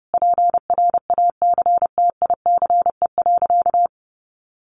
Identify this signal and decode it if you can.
Morse code, heard as a keyed tone, about 28 wpm, 700 Hz. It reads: PRACTICE.
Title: Sentences from Top 900 Words 30wpm